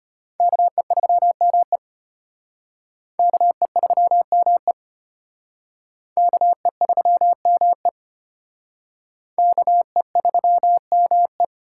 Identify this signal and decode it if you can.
Morse code, heard as a keyed tone, about 31 wpm, 700 Hz. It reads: KE3ME KE3ME KE3ME KE3ME